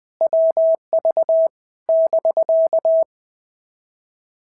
Morse code, heard as a keyed tone, about 19 wpm, 650 Hz. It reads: WV <BK>
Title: POTA QSO Practice - 20wpm